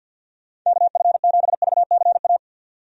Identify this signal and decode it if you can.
Morse code, heard as a keyed tone, about 52 wpm, 700 Hz. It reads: KF7VCA
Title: USA Call Signs - 6 Characters - 50wpm